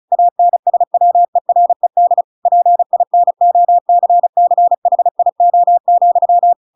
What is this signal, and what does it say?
Morse code, heard as a keyed tone, about 34 wpm, 700 Hz. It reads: ANSWERED PINOCCHIO,
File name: Pinocchio / Pinocchio ch7 35 35 ik1hge